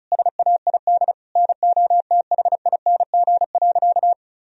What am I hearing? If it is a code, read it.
Morse code, heard as a keyed tone, about 36 wpm, 700 Hz. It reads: SAID NOTHING.
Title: Sentences from Top 1000 Words - Review Part 4 - 35wpm